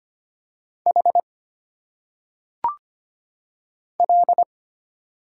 Morse code, heard as a keyed tone, about 25 wpm, 700 Hz. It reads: H L
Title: Single Letters - Phonetic Pronunciation 25wpm